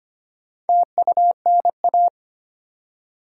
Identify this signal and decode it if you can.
Morse code, heard as a keyed tone, about 25 wpm, 700 Hz. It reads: TUNA
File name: practice / 25wpm / Difficult Words with Hints - 25wpm